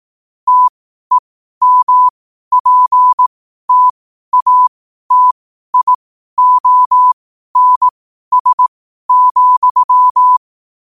Morse code, heard as a keyed tone, about 18 wpm, spaced as for 12 wpm, 1000 Hz. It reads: TEMPTATIONS,
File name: qotdfiles / 12wpm